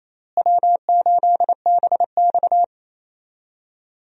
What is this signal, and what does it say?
Morse code, heard as a keyed tone, about 28 wpm, 700 Hz. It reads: W8BX
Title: USA Call Signs - 4 Characters - Repeated 3x - 28wpm